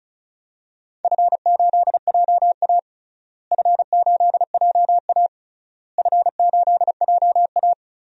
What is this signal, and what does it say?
Morse code, heard as a keyed tone, about 36 wpm, 700 Hz. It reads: F8JA F8JA F8JA
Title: DXCC-100 Call Signs - 4 Characters - Repeated 3x - 35wpm